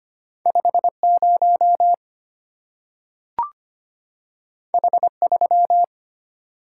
Morse code, heard as a keyed tone, about 25 wpm, 700 Hz. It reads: 50 H3
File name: Two Letter-Number Combinations - 25wpm